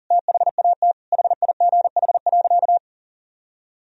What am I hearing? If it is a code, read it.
Morse code, heard as a keyed tone, about 40 wpm, 700 Hz. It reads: THAT HIGH.
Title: Sentences from Top 1000 Words - Review Part 1 - 40wpm